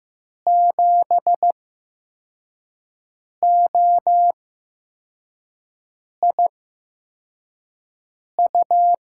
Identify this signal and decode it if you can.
Morse code, heard as a keyed tone, about 15 wpm, spaced as for 4 wpm, 700 Hz. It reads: 7OIU
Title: USA Call Signs - 6 Characters - 8x Character Spacing - 15wpm